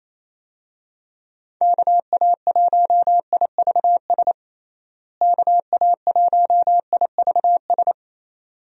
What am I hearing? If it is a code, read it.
Morse code, heard as a keyed tone, about 28 wpm, 700 Hz. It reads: KA1IVS KA1IVS
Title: USA Call Signs - 6 Characters - Repeated 2x - 28wpm